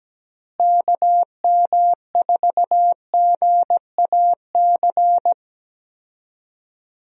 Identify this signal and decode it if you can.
Morse code, heard as a keyed tone, about 17 wpm, 700 Hz. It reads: KM4GAC
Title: USA Call Signs - 6 Characters - Repeated 3x - 17wpm